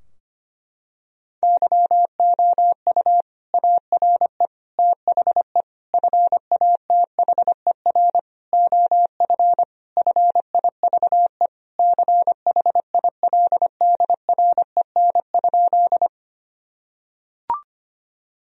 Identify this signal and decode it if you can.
Morse code, heard as a keyed tone, about 25 wpm, 700 Hz. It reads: YOU ARE THE FATHER OF FIVE CHILDREN?
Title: Sentences from Top 1000 Words - Review Part 2 - 25wpm